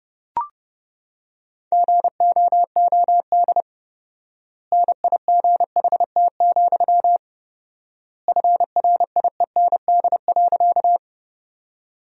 Morse code, heard as a keyed tone, about 30 wpm, 700 Hz. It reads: GOOD NIGHT, FRIEND.
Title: 4x Word-Spacing - Sentences from Top 300 Words - 30wpm